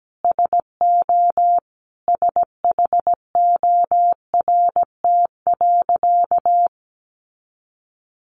Morse code, heard as a keyed tone, about 17 wpm, 700 Hz. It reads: SO SHORT.
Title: Sentences from Top 500 Words 17wpm